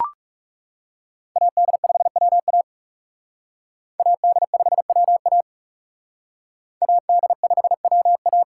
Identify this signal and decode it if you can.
Morse code, heard as a keyed tone, about 37 wpm, 700 Hz. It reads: AD5WA AD5WA AD5WA
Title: Speed Racing - USA 2x2 Call Signs 30wpm